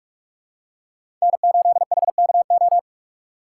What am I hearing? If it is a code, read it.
Morse code, heard as a keyed tone, about 44 wpm, 700 Hz. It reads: N8HKO